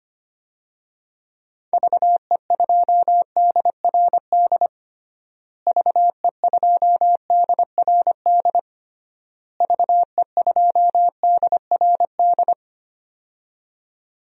Morse code, heard as a keyed tone, about 25 wpm, 700 Hz. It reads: VE2DRD VE2DRD VE2DRD
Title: DXCC-100 Call Signs - 6 Characters - Repeated 3x - 25wpm